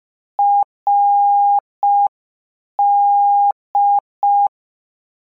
Morse code, heard as a keyed tone, about 5 wpm, 810 Hz. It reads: RD